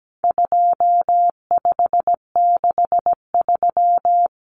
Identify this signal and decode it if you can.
Morse code, heard as a keyed tone, about 17 wpm, 700 Hz. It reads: 2563
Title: Common Cut Numbers - 4-Digit 17wpm